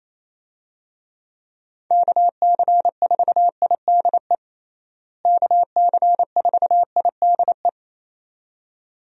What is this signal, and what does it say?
Morse code, heard as a keyed tone, about 28 wpm, 700 Hz. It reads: KC4IDE KC4IDE
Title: USA Call Signs - 6 Characters - Repeated 2x - 28wpm